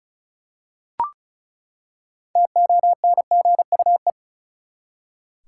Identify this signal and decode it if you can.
Morse code, heard as a keyed tone, about 36 wpm, 700 Hz. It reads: TONGUE
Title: Six Letter Words - 35wpm